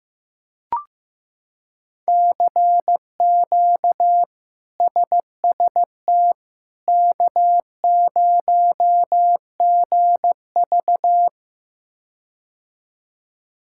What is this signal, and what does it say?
Morse code, heard as a keyed tone, about 15 wpm, 700 Hz. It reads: CQ SST K0GV
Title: K1USN SST QSO Practice - 15wpm